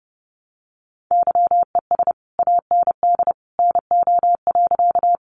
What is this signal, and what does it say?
Morse code, heard as a keyed tone, about 30 wpm, 700 Hz. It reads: YES AND NO.
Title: Sentences from Top 500 Words 30wpm